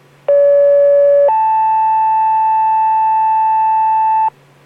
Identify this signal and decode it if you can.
Two-tone page, heard as a one-second tone, then a three-second tone, about 570 then 880 Hz, with noise. It is heard from 0.3 to 4.3 s.